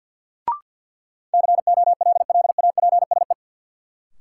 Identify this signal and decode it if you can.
Morse code, heard as a keyed tone, about 52 wpm, 700 Hz. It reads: COLLAPSE
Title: Sets of 5 Words Encore - New Word Drill - 50wpm